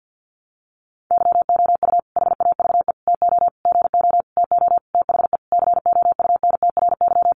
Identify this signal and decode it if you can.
Morse code, heard as a keyed tone, about 50 wpm, 700 Hz. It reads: YOU HAVE TO GO TO THE COUNTRY
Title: Sentences from Top 1300 Words 50wpm